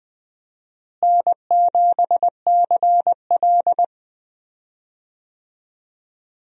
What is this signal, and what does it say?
Morse code, heard as a keyed tone, about 20 wpm, 700 Hz. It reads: N7CL
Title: USA Call Signs - 4 Characters - Repeated 3x - 20wpm